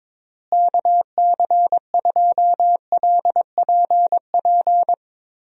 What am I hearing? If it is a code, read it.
Morse code, heard as a keyed tone, about 22 wpm, 700 Hz. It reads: KC2LPP
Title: USA Call Signs - 6 Characters - Repeated 3x - 22wpm